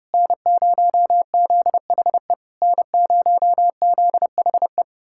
Morse code, heard as a keyed tone, about 33 wpm, 700 Hz. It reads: N0ZHE N0ZHE